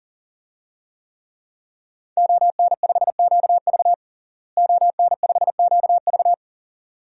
Morse code, heard as a keyed tone, about 40 wpm, 700 Hz. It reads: ON5QV ON5QV
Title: DXCC-100 Call Signs - 5 Characters - Repeated 2x - 40wpm